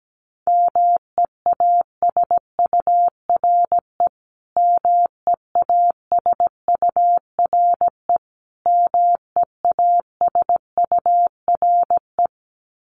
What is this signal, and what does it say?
Morse code, heard as a keyed tone, about 17 wpm, 700 Hz. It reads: MEASURE MEASURE MEASURE
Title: Top 500 Words Repeated 17wpm